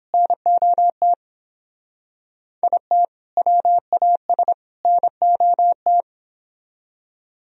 Morse code, heard as a keyed tone, about 29 wpm, 700 Hz. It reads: NOT IT WAS NOT